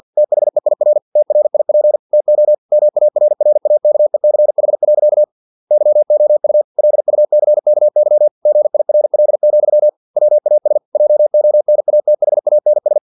Morse code, heard as a keyed tone, about 50 wpm, 600 Hz. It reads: THEIR TRIP TO MARRAKECH. YOU LUCKY GIRL, WAS JONATHANS
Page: Never 49wpm